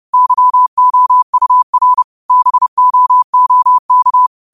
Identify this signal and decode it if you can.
Morse code, heard as a keyed tone, about 30 wpm, 1000 Hz. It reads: YOUR BOOK
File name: qotdfiles / 30wpm